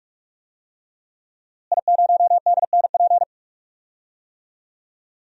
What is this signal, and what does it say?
Morse code, heard as a keyed tone, about 46 wpm, 700 Hz. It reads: I0DNP
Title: DXCC-100 Call Signs - 5 Characters - Repeated 3x - 45wpm